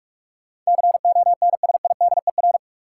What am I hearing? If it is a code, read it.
Morse code, heard as a keyed tone, about 41 wpm, 700 Hz. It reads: CONSIDER
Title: Sentences from Top 600 Words - New Word Drill - 45wpm